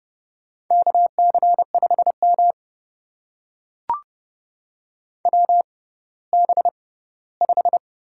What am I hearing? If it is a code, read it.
Morse code, heard as a keyed tone, about 30 wpm, spaced as for 17 wpm, 700 Hz. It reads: KC5M WB5
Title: USA Call Signs - 6x Character Spacing - 30wpm